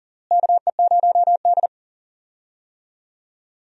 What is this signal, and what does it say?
Morse code, heard as a keyed tone, about 40 wpm, 700 Hz. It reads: KE0D
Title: USA Call Signs - 4 Characters - Repeated 3x - 40wpm